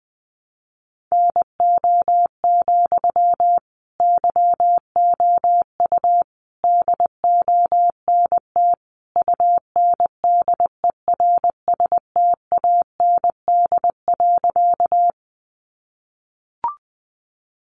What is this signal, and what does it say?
Morse code, heard as a keyed tone, about 20 wpm, 700 Hz. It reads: NO, YOU DONT UNDERSTAND.
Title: Sentences from Top 600 Words 20wpm